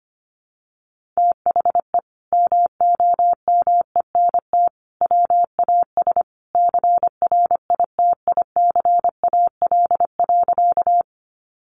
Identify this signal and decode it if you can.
Morse code, heard as a keyed tone, about 25 wpm, 700 Hz. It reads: THE MOMENT WAS CRITICAL.
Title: Sentences from Top 1100 Words 25wpm